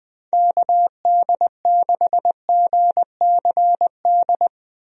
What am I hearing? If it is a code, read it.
Morse code, heard as a keyed tone, about 21 wpm, 700 Hz. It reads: KD6GCD
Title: USA Call Signs - 6 Characters - Repeated 2x - 20wpm